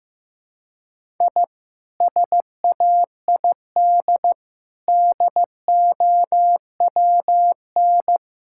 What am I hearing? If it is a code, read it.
Morse code, heard as a keyed tone, about 15 wpm, 700 Hz. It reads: I SAID DOWN
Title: Sentences from Top 1000 Words - Review Part 5 - 15wpm